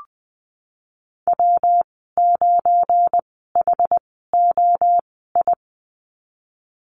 Morse code, heard as a keyed tone, about 20 wpm, spaced as for 15 wpm, 700 Hz. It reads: W9HOI
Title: 186 Lesson 30g - Letter b - Call Signs 20wpm